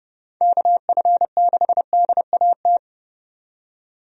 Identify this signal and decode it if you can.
Morse code, heard as a keyed tone, about 30 wpm, 700 Hz. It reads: KF6DAT
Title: USA Call Signs - 6 Characters - 30wpm